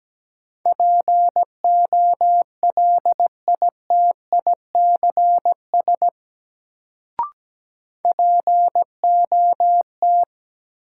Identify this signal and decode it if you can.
Morse code, heard as a keyed tone, about 17 wpm, 700 Hz. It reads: POLITICS POT
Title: Sentences from Top 1000 Words - New Word Drill - 17wpm